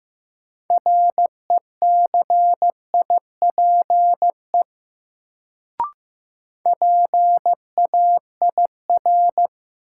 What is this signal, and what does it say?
Morse code, heard as a keyed tone, about 15 wpm, 700 Hz. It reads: RECIPE PAIR
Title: Sentences from Top 1200 Words - New Word Drill - 15wpm